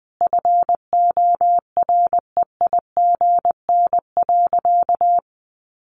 Morse code, heard as a keyed tone, about 20 wpm, 700 Hz. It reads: FOREIGN.
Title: Sentences from Top 1400 Words 20wpm